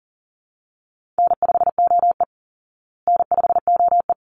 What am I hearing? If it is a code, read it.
Morse code, heard as a keyed tone, about 37 wpm, 700 Hz. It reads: N5OE N5OE
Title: Easy Call Signs Repeated 2x - 40wpm